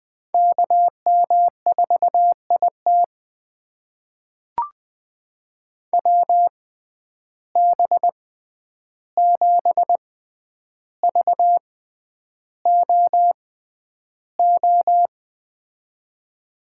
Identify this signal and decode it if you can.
Morse code, heard as a keyed tone, about 20 wpm, spaced as for 10 wpm, 700 Hz. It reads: KM4IT WB7VOO
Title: USA Call Signs - 6x Character Spacing - 20wpm